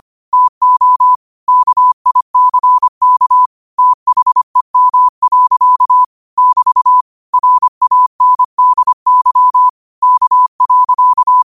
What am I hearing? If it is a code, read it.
Morse code, heard as a keyed tone, about 25 wpm, 1000 Hz. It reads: TO KICK THEM. <BT> RANDY K.
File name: qotdfiles / 25wpm